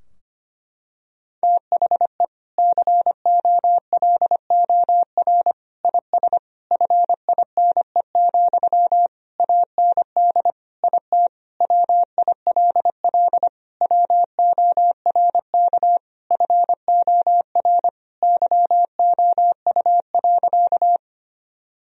Morse code, heard as a keyed tone, about 25 wpm, 700 Hz. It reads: THE COLOR IS FINE, AND IT WILL WORK FOR YOU.
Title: Sentences from Top 1000 Words - Review Part 2 - 25wpm